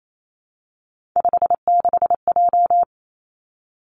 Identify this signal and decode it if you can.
Morse code, heard as a keyed tone, about 28 wpm, 700 Hz. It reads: 56J
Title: Three Challenging Letter-Number Combinations 28wpm